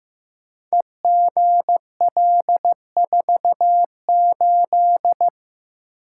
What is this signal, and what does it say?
Morse code, heard as a keyed tone, about 15 wpm, 700 Hz. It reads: EGL48
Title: Code Groups - 5 Letter:Numbers - 15wpm